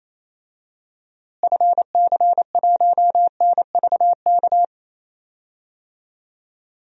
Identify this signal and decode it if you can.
Morse code, heard as a keyed tone, about 28 wpm, 700 Hz. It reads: FC1NVK
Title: DXCC-100 Call Signs - 6 Characters - Repeated 3x - 28wpm